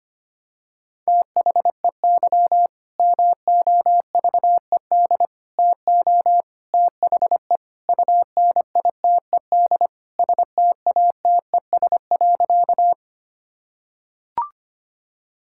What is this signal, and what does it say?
Morse code, heard as a keyed tone, about 25 wpm, 700 Hz. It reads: THEY MOVED TO THE UNITED STATES.
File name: practice / 25wpm / Sentences with 4-Word Phrases - 25wpm